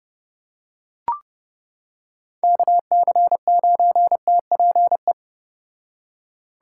Morse code, heard as a keyed tone, about 30 wpm, 700 Hz. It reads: KC9TPE